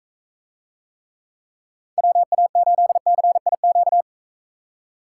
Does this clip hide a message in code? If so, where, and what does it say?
Yes: Morse code from 2.0 to 4.0 s, heard as a keyed tone, about 44 wpm, 700 Hz. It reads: WA8CIQ